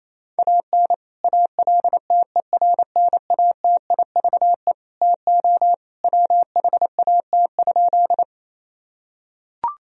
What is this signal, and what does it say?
Morse code, heard as a keyed tone, about 28 wpm, 700 Hz. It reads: AN ALTERNATIVE TO WHAT?
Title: Sentences from Top 1000 Words - Review Part 4 - 28wpm